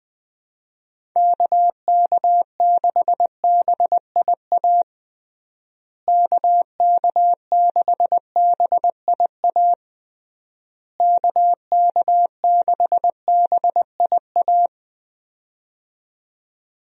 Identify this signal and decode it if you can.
Morse code, heard as a keyed tone, about 20 wpm, 700 Hz. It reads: KK6BIA KK6BIA KK6BIA
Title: USA Call Signs - 6 Characters - Repeated 3x - 20wpm